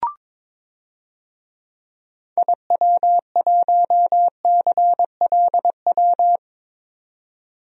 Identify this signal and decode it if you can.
Morse code, heard as a keyed tone, about 22 wpm, 700 Hz. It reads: IW1CLW
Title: DXCC-100 Call Signs - 6 Characters - 22wpm